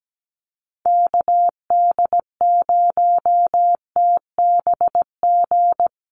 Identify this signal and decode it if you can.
Morse code, heard as a keyed tone, about 17 wpm, 700 Hz. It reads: KD0TBG
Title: USA Call Signs Repeated 3x 17wpm